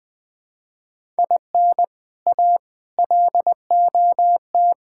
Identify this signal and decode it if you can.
Morse code, heard as a keyed tone, about 20 wpm, 690 Hz. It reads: IN A LOT
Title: Build A Phrase - 4-Word Phrases - 20wpm